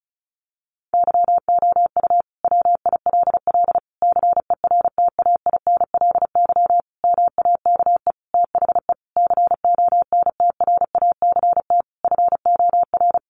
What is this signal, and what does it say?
Morse code, heard as a keyed tone, about 34 wpm, 700 Hz. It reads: YOU WILL CERTAINLY MAKE THE CONTRACT FOR
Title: Sentences from Top 1400 Words 35wpm